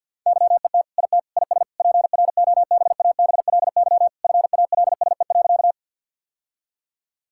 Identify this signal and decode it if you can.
Morse code, heard as a keyed tone, about 50 wpm, 700 Hz. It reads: YET IT IS PROBABLY FALSE.
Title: Sentences from Top 1600 Words 50wpm